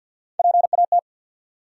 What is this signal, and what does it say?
Morse code, heard as a keyed tone, about 52 wpm, 700 Hz. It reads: PAT
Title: USA Common Names - Max 4 Letters - 50wpm